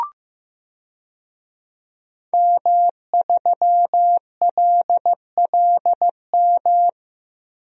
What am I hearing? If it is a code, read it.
Morse code, heard as a keyed tone, about 15 wpm, 700 Hz. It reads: M3LLM
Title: DXCC-100 Call Signs - 5 Characters - Repeated 2x - 15wpm